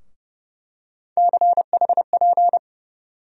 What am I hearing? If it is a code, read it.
Morse code, heard as a keyed tone, about 30 wpm, 700 Hz. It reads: CHP
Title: USA Call Sign Postfixes - 3 Letters 30wpm